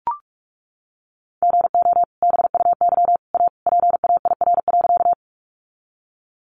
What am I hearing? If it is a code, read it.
Morse code, heard as a keyed tone, about 44 wpm, 700 Hz. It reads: GO BUY A PAIR.
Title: Sentences from Top 1200 Words 45wpm